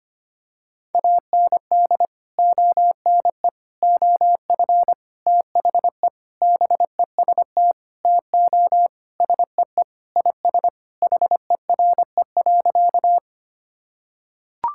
Morse code, heard as a keyed tone, about 25 wpm, 700 Hz. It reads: AND ONE OF THE BEST TO SEE IS HERE.
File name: practice / 25wpm / Sentences with 4-Word Phrases - 25wpm